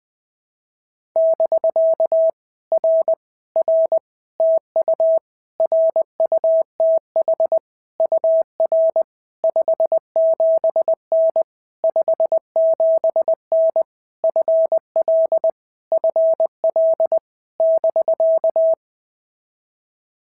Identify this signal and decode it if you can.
Morse code, heard as a keyed tone, about 21 wpm, 650 Hz. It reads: <BK> R R TU RUTH UR 57N 57N FL FL <BK>
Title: POTA QSO Practice - 20wpm